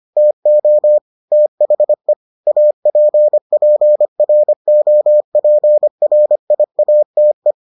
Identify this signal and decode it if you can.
Morse code, heard as a keyed tone, about 25 wpm, 600 Hz. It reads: TO THE APPROPRIATE